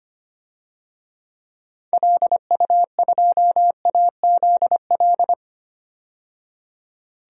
Morse code, heard as a keyed tone, about 25 wpm, 700 Hz. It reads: LU2AZL
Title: DXCC-100 Call Signs - 6 Characters - 25wpm